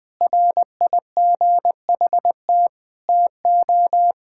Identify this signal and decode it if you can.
Morse code, heard as a keyed tone, about 20 wpm, 700 Hz. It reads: RIGHT TO
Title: Sentences from Top 1000 Words - Review Part 5 - 20wpm